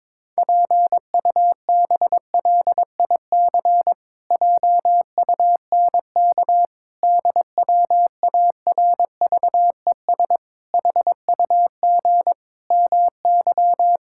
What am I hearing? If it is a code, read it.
Morse code, heard as a keyed tone, about 22 wpm, 700 Hz. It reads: PUBLIC JUNK DWARVES HUG MY
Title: Pangram Challenge v2 - 22wpm